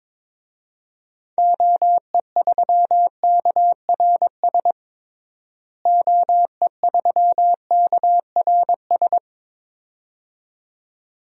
Morse code, heard as a keyed tone, about 22 wpm, 700 Hz. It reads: OE3KRS OE3KRS
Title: DXCC-100 Call Signs - 6 Characters - Repeated 2x - 22wpm